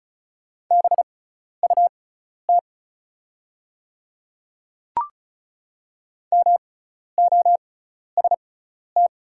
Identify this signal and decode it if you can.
Morse code, heard as a keyed tone, about 36 wpm, spaced as for 12 wpm, 700 Hz. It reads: BUT MOST
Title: 6x Character Spacing - Top 100 Words - 35wpm